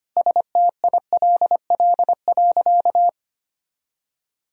Morse code, heard as a keyed tone, about 25 wpm, 700 Hz. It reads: STILL.
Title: Sentences from Top 400 Words and Abbreviations 25wpm